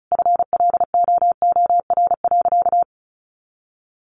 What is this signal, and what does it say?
Morse code, heard as a keyed tone, about 34 wpm, 700 Hz. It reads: FLOOR.
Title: Additional Easy Phrases 35wpm